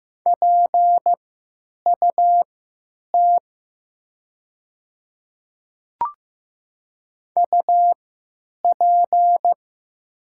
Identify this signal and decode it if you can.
Morse code, heard as a keyed tone, about 15 wpm, spaced as for 9 wpm, 700 Hz. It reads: PUT UP
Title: 3x Character Spacing - Top 200 Words - 15wpm